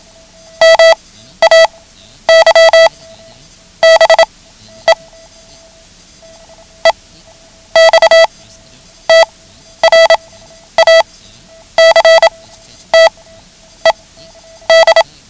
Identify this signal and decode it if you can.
Morse code, heard as a keyed tone, about 27 wpm, spaced as for 10 wpm, 660 Hz. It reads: MAY BE EXTRACTED
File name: ham radio / CodeQuick / Code Quick 09